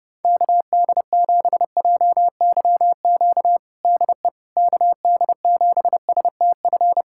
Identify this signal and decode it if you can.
Morse code, heard as a keyed tone, about 32 wpm, 700 Hz. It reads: KD7JYQ DE KD7STF
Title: USA Callsign DE Callsign 30wpm